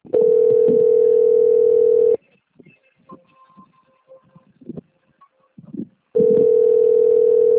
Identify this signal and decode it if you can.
Ringback tone (North America), 440 + 480 Hz.